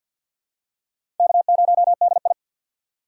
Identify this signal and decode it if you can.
Morse code, heard as a keyed tone, about 53 wpm, 700 Hz. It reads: K0DI
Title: USA Call Signs - 4 Characters - 3x Character Spacing - 50wpm